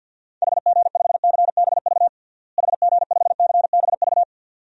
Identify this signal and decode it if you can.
Morse code, heard as a keyed tone, about 53 wpm, 700 Hz. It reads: HG5CBV HG5CBV
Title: DXCC-100 Call Signs - 6 Characters - Repeated 2x - 50wpm